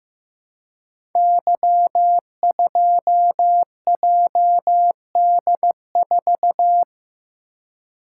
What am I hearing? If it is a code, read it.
Morse code, heard as a keyed tone, about 15 wpm, 700 Hz. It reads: Y2JD4
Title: Code Groups - 5 Letter:Numbers - 15wpm